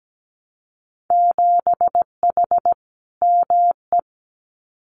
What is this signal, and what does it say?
Morse code, heard as a keyed tone, about 17 wpm, 700 Hz. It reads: 7H ME
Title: Winter Field Day Exchanges Encore 17wpm